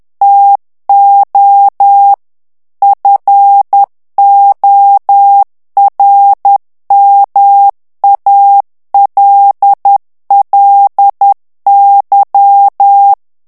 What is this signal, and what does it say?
Morse code, heard as a keyed tone, about 10 wpm, 800 Hz. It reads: TO FORMALLY
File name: ANGLAIS texte clair 3 vitesse 10 mots minute